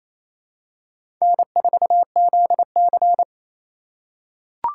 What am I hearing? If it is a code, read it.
Morse code, heard as a keyed tone, about 28 wpm, 700 Hz. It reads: N4ZC
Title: USA Call Signs - 4 Characters - 3x Character Spacing - 28wpm